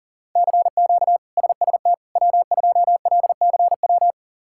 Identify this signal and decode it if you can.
Morse code, heard as a keyed tone, about 40 wpm, 700 Hz. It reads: CQ SST W2LCW
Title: K1USN SST QSO Practice - 40wpm